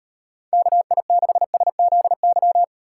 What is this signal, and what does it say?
Morse code, heard as a keyed tone, about 38 wpm, 700 Hz. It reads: KI6SZY